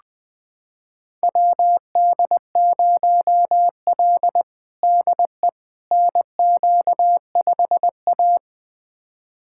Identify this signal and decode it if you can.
Morse code, heard as a keyed tone, about 21 wpm, 700 Hz. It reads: WD0L DE NQ5A
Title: USA Callsign DE Callsign - 4 Characters - 20wpm